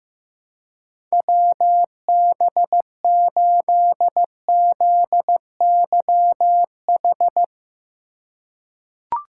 Morse code, heard as a keyed tone, about 15 wpm, 700 Hz. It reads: WB8ZYH
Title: USA Call Signs - 2x Character Spacing - 15wpm